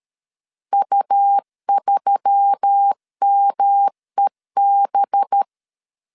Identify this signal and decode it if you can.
Morse code, heard as a keyed tone, about 13 wpm, 800 Hz. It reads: U3MEB
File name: CW / DET / 1 6 1500C S B 12wpm 3431